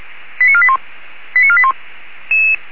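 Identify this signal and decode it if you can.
Five-tone selective call (ZVEI), heard as quick runs of five tones, tones of about 70 ms. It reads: 89481 89481
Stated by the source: ZVEI five-tone selective call received on SDR.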